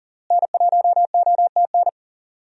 Morse code, heard as a keyed tone, about 40 wpm, 700 Hz. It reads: N1OTN
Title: USA Call Signs - 5 Characters - Repeated 2x - 40wpm